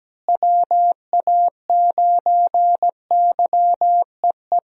Morse code, heard as a keyed tone, about 17 wpm, 700 Hz. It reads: WA9YEE
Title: USA Call Signs - 6 Characters - Repeated 3x - 17wpm